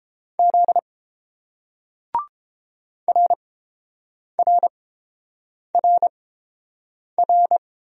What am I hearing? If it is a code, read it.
Morse code, heard as a keyed tone, about 30 wpm, spaced as for 7 wpm, 700 Hz. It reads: Z RRRR